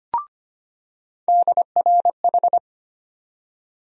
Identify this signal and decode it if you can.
Morse code, heard as a keyed tone, about 25 wpm, 700 Hz. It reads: DRH
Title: USA Call Sign Postfixes - 3 Letters 25wpm